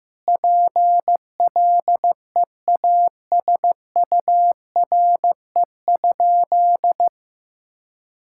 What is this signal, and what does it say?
Morse code, heard as a keyed tone, about 15 wpm, 700 Hz. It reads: PLEASURE?
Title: Sentences from Top 1600 Words 15wpm